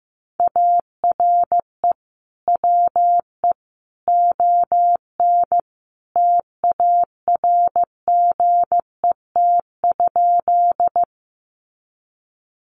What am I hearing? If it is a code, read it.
Morse code, heard as a keyed tone, about 15 wpm, 700 Hz. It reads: ARE WE ON TARGET?
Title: Sentences from Top 1200 Words 15wpm